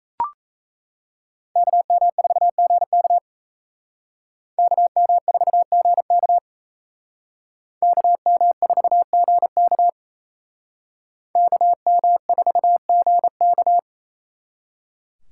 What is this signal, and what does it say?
Morse code, heard as a keyed tone, about 35 wpm, 700 Hz. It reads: KM4GK KM4GK KM4GK KM4GK
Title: Speed Racing - USA Call Signs 28wpm